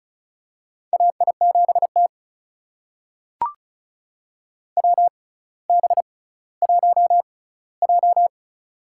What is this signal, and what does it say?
Morse code, heard as a keyed tone, about 34 wpm, spaced as for 18 wpm, 700 Hz. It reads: AI7T WB1J
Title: USA Call Signs - 4 Characters - 6x Character Spacing - 35wpm